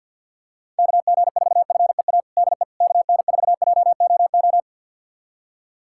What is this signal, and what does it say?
Morse code, heard as a keyed tone, about 51 wpm, 700 Hz. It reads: KG4FEA DE KN4JOO
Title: USA Callsign DE Callsign - 6 Characters - 50wpm